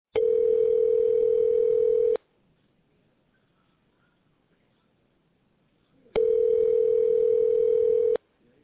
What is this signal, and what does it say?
Ringback tone (North America), 440 + 480 Hz.